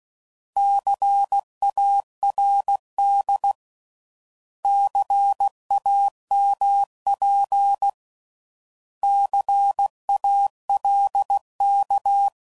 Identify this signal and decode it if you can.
Morse code, heard as a keyed tone, about 16 wpm, 800 Hz. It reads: CARD CAMP CALK
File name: Prefixes / By First Letter / Cs